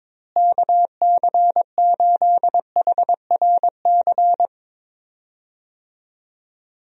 Morse code, heard as a keyed tone, about 22 wpm, 700 Hz. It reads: KC8HRC
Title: USA Call Signs - 6 Characters - Repeated 3x - 22wpm